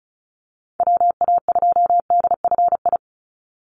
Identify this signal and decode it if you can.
Morse code, heard as a keyed tone, about 34 wpm, 700 Hz. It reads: WA2DFI
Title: Call Signs Repeated 2x - 35wpm